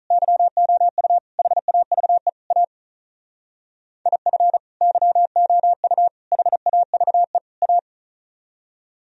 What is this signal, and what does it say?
Morse code, heard as a keyed tone, about 45 wpm, 700 Hz. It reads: YOU HAVE A IF YOU HAVE A